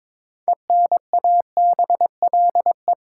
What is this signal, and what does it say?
Morse code, heard as a keyed tone, about 22 wpm, 700 Hz. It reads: ENABLE
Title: Sentences from Top 1500 Words - New Word Drill Alt - 22wpm